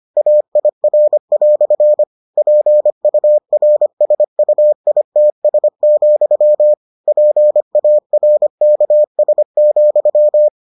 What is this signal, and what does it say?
Morse code, heard as a keyed tone, about 25 wpm, 600 Hz. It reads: AIR" PURSUITS, PARKS,